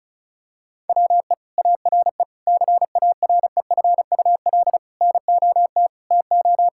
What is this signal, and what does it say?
Morse code, heard as a keyed tone, about 36 wpm, 700 Hz. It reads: WE ARE CAREFUL NOT TO
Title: Sentences from Top 1600 Words 35wpm